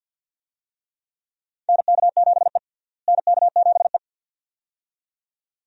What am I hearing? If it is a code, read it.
Morse code, heard as a keyed tone, about 53 wpm, 700 Hz. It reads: NK7E NK7E
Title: USA Call Signs - 4 Characters - Repeated 2x - 50wpm